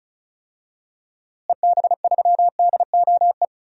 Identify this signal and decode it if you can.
Morse code, heard as a keyed tone, about 34 wpm, 700 Hz. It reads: EB3DOE
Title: DXCC-100 Call Signs - 6 Characters - Repeated 3x - 35wpm